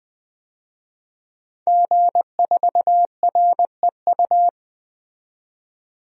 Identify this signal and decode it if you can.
Morse code, heard as a keyed tone, about 20 wpm, 700 Hz. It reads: G4REU
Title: DXCC-100 Call Signs - 5 Characters - 20wpm